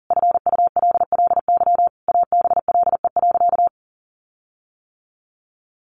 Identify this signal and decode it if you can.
Morse code, heard as a keyed tone, about 40 wpm, 700 Hz. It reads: FULLY ABLE.
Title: Sentences from Top 1300 Words 40wpm